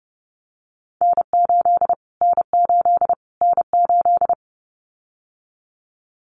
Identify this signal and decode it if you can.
Morse code, heard as a keyed tone, about 29 wpm, 700 Hz. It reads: N8 N8 N8
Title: Callsign Prefixes for Top 25 DXCC - Repeated 3x - 30wpm